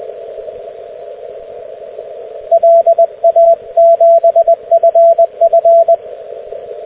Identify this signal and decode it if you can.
Morse code, heard as a keyed tone, about 20 wpm, 670 Hz, with radio noise. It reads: LA7FF
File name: cw / ubacw dx